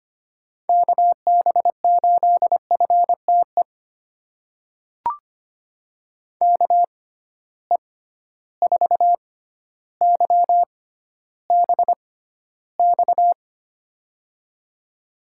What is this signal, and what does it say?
Morse code, heard as a keyed tone, about 25 wpm, spaced as for 13 wpm, 700 Hz. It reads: KB8FTE KE4YBX
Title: USA Call Signs - 6 Characters - 6x Character Spacing - 25wpm